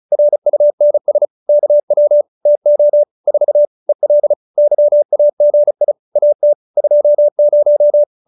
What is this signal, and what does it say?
Morse code, heard as a keyed tone, about 34 wpm, 590 Hz. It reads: RUNS KW TO 4 EL YAGI AT 20